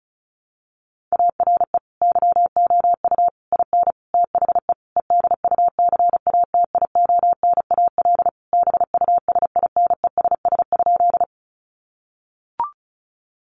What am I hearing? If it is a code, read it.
Morse code, heard as a keyed tone, about 34 wpm, 700 Hz. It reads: ARE YOU IN THE EDUCATIONAL BUSINESS?
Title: Sentences from Top 1200 Words 35wpm